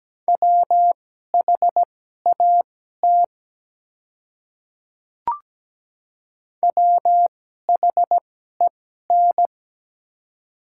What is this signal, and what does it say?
Morse code, heard as a keyed tone, about 17 wpm, spaced as for 12 wpm, 700 Hz. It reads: WHAT WHEN
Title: 2x Character Spacing - Top 100 Words - 17wpm